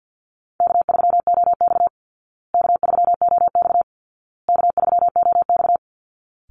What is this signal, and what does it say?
Morse code, heard as a keyed tone, about 48 wpm, 700 Hz. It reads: K3OX K3OX K3OX
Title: Easy Call Signs Repeated 3x - 50wpm